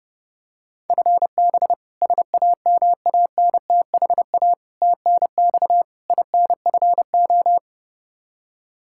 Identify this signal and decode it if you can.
Morse code, heard as a keyed tone, about 30 wpm, 700 Hz. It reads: FB SAMANTHA TNX INFO
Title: K1USN SST QSO Practice 30wpm